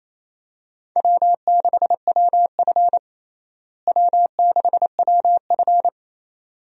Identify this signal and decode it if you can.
Morse code, heard as a keyed tone, about 28 wpm, 700 Hz. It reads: W6WF W6WF
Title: USA Call Signs - 4 Characters - Repeated 2x - 28wpm